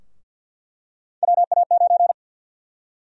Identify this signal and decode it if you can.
Morse code, heard as a keyed tone, about 53 wpm, 700 Hz. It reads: WA9
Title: USA Call Sign Prefixes - 50wpm